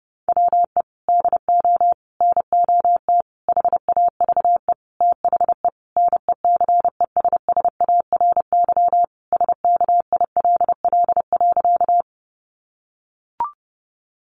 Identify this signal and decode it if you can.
Morse code, heard as a keyed tone, about 30 wpm, 700 Hz. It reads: WE DO NOT HAVE THE NECESSARY SKILL.
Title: Sentences from Top 1200 Words 30wpm